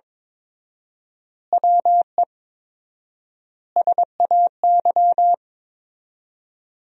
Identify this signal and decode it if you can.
Morse code, heard as a keyed tone, about 22 wpm, 700 Hz. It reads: WE SAY
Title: 4x Word-Spacing - Sentences from Top 200 Words - 22wpm